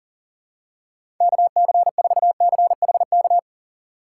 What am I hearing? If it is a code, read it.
Morse code, heard as a keyed tone, about 40 wpm, 700 Hz. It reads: KC4CHK
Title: USA Call Signs - 6 Characters - 6x Character Spacing - 40wpm